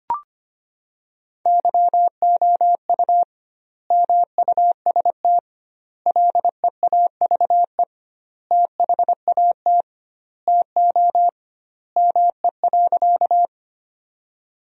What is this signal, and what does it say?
Morse code, heard as a keyed tone, about 25 wpm, 700 Hz. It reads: YOU MUST LEAVE THAT TO ME.
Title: 2x Word-Spacing - Sentences from Top 400 Words - 25wpm